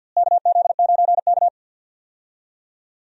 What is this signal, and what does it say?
Morse code, heard as a keyed tone, about 52 wpm, 700 Hz. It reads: KZ9K
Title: USA Call Signs - 4 Characters - Repeated 3x - 50wpm